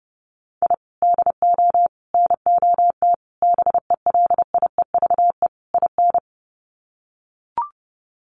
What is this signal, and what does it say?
Morse code, heard as a keyed tone, about 30 wpm, 700 Hz. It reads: I DO NOT BELIEVE IN
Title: Sets of 5 Words Encore 30wpm